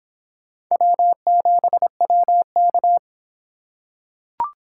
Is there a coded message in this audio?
Yes: Morse code from 0.7 to 3.0 s, heard as a keyed tone, about 26 wpm, 700 Hz. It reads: W7WK